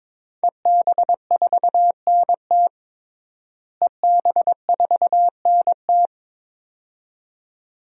Morse code, heard as a keyed tone, about 21 wpm, 700 Hz. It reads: EB4NT EB4NT
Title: DXCC-100 Call Signs - 5 Characters - Repeated 2x - 22wpm